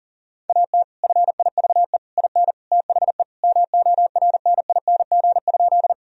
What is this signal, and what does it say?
Morse code, heard as a keyed tone, about 40 wpm, 700 Hz. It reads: AT FIVE IN THE MORNING?
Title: Sentences from Top 1000 Words - Review Part 2 - 40wpm